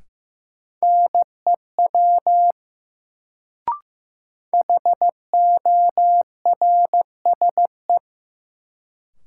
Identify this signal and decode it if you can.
Morse code, heard as a keyed tone, about 15 wpm, 700 Hz. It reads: NEW HORSE
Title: Sentences from Top 300 Words - New Word Drill - 15wpm